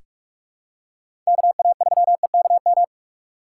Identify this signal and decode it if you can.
Morse code, heard as a keyed tone, about 44 wpm, 690 Hz. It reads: KA3EKM
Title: USA Call Signs - 6 Characters - 2x Character Spacing - 45wpm